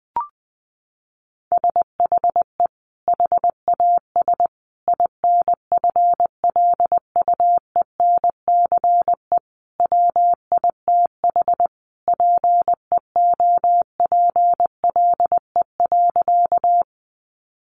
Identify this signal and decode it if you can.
Morse code, heard as a keyed tone, about 20 wpm, 700 Hz. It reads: SHE HAS INFLUENCE WITH PEOPLE.
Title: Sentences from Top 1100 Words 20wpm